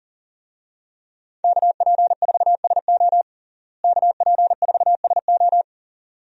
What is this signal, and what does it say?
Morse code, heard as a keyed tone, about 40 wpm, 700 Hz. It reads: KP4SO KP4SO
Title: DXCC-100 Call Signs - 5 Characters - Repeated 2x - 40wpm